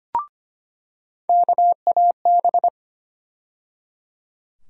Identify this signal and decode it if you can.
Morse code, heard as a keyed tone, about 25 wpm, 700 Hz. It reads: KAB
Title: USA Call Sign Postfixes - 3 Letters 25wpm